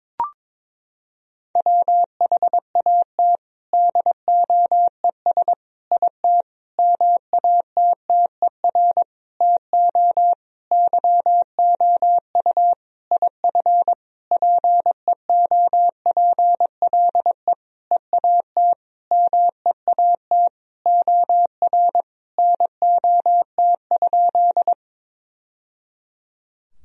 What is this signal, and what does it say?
Morse code, heard as a keyed tone, about 22 wpm, 700 Hz. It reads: WHAT DOES IT MATTER TO YOU IF PEOPLE EAT MEAT OR NOT?
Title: Sentences from Top 1000 Words - Review Part 3 - 22wpm